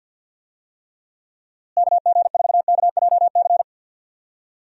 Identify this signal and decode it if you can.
Morse code, heard as a keyed tone, about 50 wpm, 700 Hz. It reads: KG4KJC
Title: USA Call Signs - 6 Characters - Repeated 2x - 50wpm